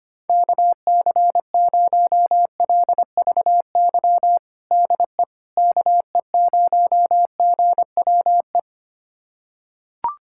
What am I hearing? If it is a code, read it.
Morse code, heard as a keyed tone, about 25 wpm, 700 Hz. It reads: KC0LVY DE KE0GWE
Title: USA Callsign DE Callsign 25wpm